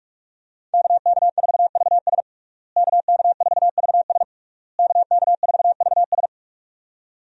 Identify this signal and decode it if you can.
Morse code, heard as a keyed tone, about 46 wpm, 700 Hz. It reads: KK4VS KK4VS KK4VS
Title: USA Call Signs - 5 Characters - Repeated 3x - 45wpm